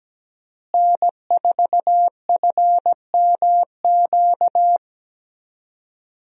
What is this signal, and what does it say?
Morse code, heard as a keyed tone, about 17 wpm, 700 Hz. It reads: N4FMQ
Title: USA Call Signs - 5 Characters - Repeated 3x - 17wpm